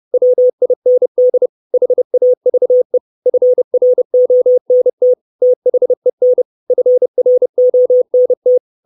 Morse code, heard as a keyed tone, about 30 wpm, 500 Hz. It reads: WIND HAVE FRONT THEN FRONT
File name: 632 Words 30WPM